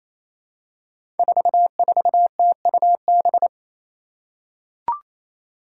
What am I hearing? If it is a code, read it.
Morse code, heard as a keyed tone, about 27 wpm, 700 Hz. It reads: 44TUB
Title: Five Challenging Letter-Number Combinations 28wpm